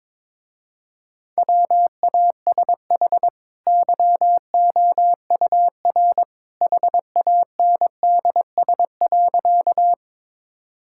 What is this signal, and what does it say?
Morse code, heard as a keyed tone, about 23 wpm, 700 Hz. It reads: WASH YOUR HANDS.
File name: practice / 22wpm / Easy Sentences with Hints - 22wpm